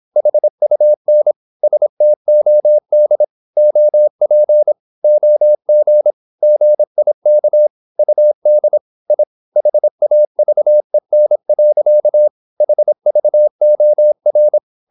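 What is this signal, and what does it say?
Morse code, heard as a keyed tone, about 26 wpm, 600 Hz. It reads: HUN STOD OP OG GIK UD I HAVEN. HVOR
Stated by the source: High-speed (QRQ) Morse at 26 wpm.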